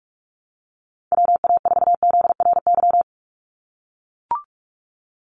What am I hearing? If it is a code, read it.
Morse code, heard as a keyed tone, about 44 wpm, 700 Hz. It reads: WA4ZRY
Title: USA Call Signs 45wpm